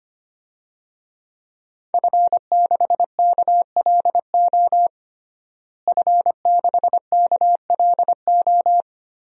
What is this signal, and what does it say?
Morse code, heard as a keyed tone, about 25 wpm, 700 Hz. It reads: F6KLO F6KLO
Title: DXCC-100 Call Signs - 5 Characters - Repeated 3x - 25wpm